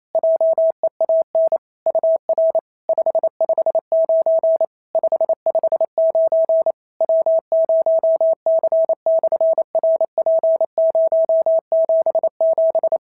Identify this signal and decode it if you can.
Morse code, heard as a keyed tone, about 28 wpm, 650 Hz. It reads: JEAN UR 559 559 W0C/RP077
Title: SOTA QSO Practice 28wpm